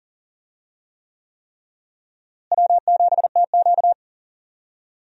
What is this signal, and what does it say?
Morse code, heard as a keyed tone, about 38 wpm, 700 Hz. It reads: W7TQ
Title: USA Call Signs - 4 Characters - Repeated 3x - 40wpm